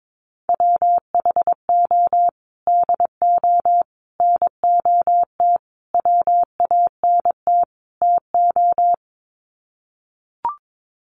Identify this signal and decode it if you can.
Morse code, heard as a keyed tone, about 22 wpm, 700 Hz. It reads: WHO DO NOT WANT TO
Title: Sets of 5 Words Encore2 22wpm